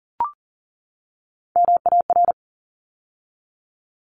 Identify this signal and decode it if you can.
Morse code, heard as a keyed tone, about 40 wpm, 700 Hz. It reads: MAR
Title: ARRL Sections 40wpm